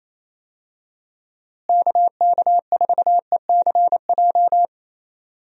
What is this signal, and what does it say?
Morse code, heard as a keyed tone, about 28 wpm, 700 Hz. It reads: KK4ECJ
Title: USA Call Signs - 6 Characters - Repeated 3x - 28wpm